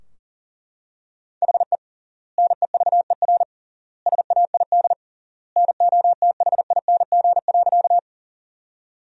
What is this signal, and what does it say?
Morse code, heard as a keyed tone, about 40 wpm, 700 Hz. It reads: HE NEVER SAID NOTHING.
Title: 3x Word-Spacing - Sentences from Top 500 Words - 40wpm